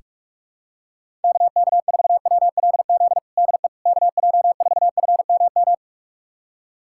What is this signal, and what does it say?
Morse code, heard as a keyed tone, about 46 wpm, 700 Hz. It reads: KK4WLZ DE KJ4FMM
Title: USA Callsign DE Callsign - 6 Characters - 45wpm